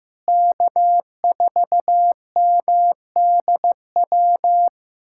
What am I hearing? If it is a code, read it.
Morse code, heard as a keyed tone, about 16 wpm, 700 Hz. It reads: K4MDW
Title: USA Callsign DE Callsign - 5 Characters - 15wpm